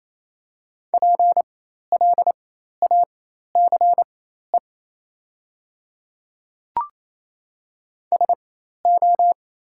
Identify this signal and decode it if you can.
Morse code, heard as a keyed tone, about 27 wpm, spaced as for 13 wpm, 700 Hz. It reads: PLACE SO
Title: 4x Character Spacing - Top 300 Words - 28wpm